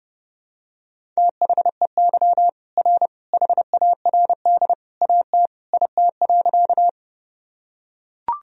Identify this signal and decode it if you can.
Morse code, heard as a keyed tone, about 32 wpm, 700 Hz. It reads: THEY R HARD AT IT.
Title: Sentences from Top 300 Words and Abbreviations 30wpm